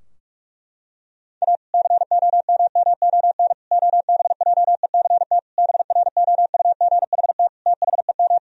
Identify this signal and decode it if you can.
Morse code, heard as a keyed tone, about 46 wpm, 700 Hz. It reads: A COMMON OBJECT BROUGHT THEM
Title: Sentences from Top 1000 Words - Review Part 3 - 45wpm